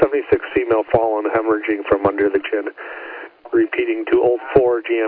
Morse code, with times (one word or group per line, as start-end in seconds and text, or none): none